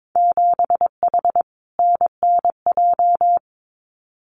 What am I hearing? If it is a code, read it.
Morse code, heard as a keyed tone, about 22 wpm, 700 Hz. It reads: 7H NNJ
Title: Winter Field Day Exchanges Encore 22wpm